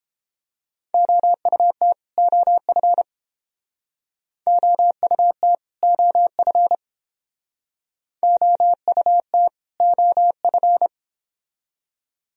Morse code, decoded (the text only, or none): OUT OF OUT OF OUT OF